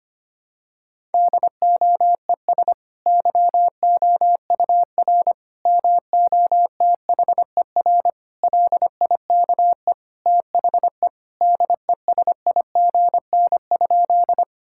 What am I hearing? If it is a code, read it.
Morse code, heard as a keyed tone, about 25 wpm, 700 Hz. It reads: DOES YOUR MOTHER LIKE THE DESIGN?
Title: Sentences from Top 1000 Words - Review Part 5 - 25wpm